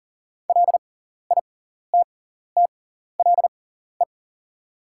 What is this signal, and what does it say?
Morse code, heard as a keyed tone, about 40 wpm, spaced as for 14 wpm, 690 Hz. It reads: LITTLE
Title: 6x Character Spacing - Top 200 Words - 40wpm